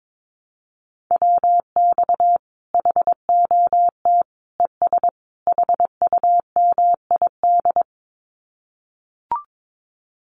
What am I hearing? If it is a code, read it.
Morse code, heard as a keyed tone, about 22 wpm, 700 Hz. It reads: WX HOT ES HUMID
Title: Common QSO Phrases 22wpm